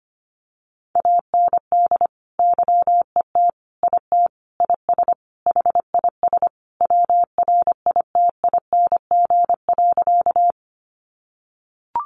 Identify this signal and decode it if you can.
Morse code, heard as a keyed tone, about 25 wpm, 700 Hz. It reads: AND YET IT IS HIS WRITING.
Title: Sentences from Top 800 Words 25wpm